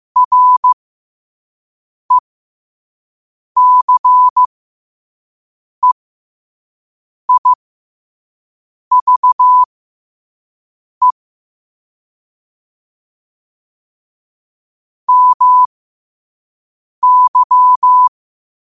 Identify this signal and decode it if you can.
Morse code, heard as a keyed tone, about 15 wpm, spaced as for 5 wpm, 1000 Hz. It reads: RECEIVE MY
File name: qotdfiles / 5wpm